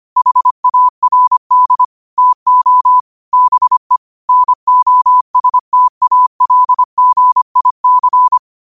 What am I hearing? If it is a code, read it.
Morse code, heard as a keyed tone, about 25 wpm, 1000 Hz. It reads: HARD TO BE NOSTALGIC
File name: qotdfiles / 25wpm